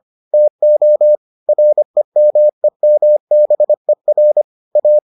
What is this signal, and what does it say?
Morse code, heard as a keyed tone, about 25 wpm, 600 Hz. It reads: TO REMEMBER A